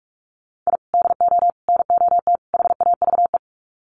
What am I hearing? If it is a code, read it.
Morse code, heard as a keyed tone, about 44 wpm, 700 Hz. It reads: I DO NOT HAVE
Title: Sets of 4 Words 45wpm